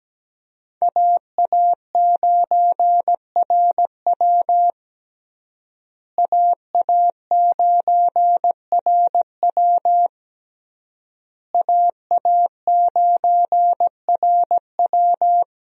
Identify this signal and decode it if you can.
Morse code, heard as a keyed tone, about 17 wpm, 700 Hz. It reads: AA9RW AA9RW AA9RW
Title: USA Call Signs - 5 Characters - Repeated 3x - 17wpm